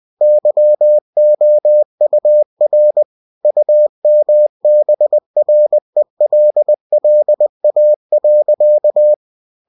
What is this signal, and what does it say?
Morse code, heard as a keyed tone, about 20 wpm, 600 Hz. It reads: YOUR UMBRELLA.